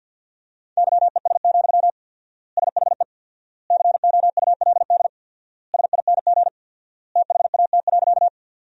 Morse code, heard as a keyed tone, about 53 wpm, 700 Hz. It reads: YES, SHE COULD SING THAT.
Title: 4x Word-Spacing - Sentences from Top 500 Words - 50wpm